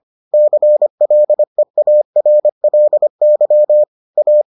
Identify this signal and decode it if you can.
Morse code, heard as a keyed tone, about 25 wpm, 600 Hz. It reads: CLEARLY A